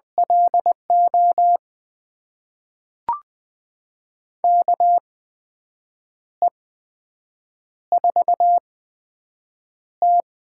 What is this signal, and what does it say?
Morse code, heard as a keyed tone, about 20 wpm, spaced as for 7 wpm, 700 Hz. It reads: LO KE4T
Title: USA Call Signs - 6 Characters - 8x Character Spacing - 20wpm